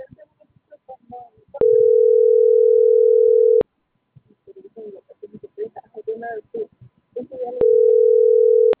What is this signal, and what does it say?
Ringback tone (North America), 440 + 480 Hz.